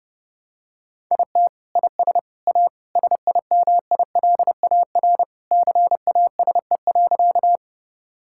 Morse code, heard as a keyed tone, about 32 wpm, 700 Hz. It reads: IT IS A SIMILAR CASE.
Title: Sentences from Top 1000 Words - Review Part 5 - 30wpm